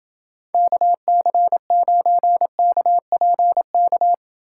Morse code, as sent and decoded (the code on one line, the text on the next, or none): -.- -.-. ----. -.- .--. -.-
KC9KPK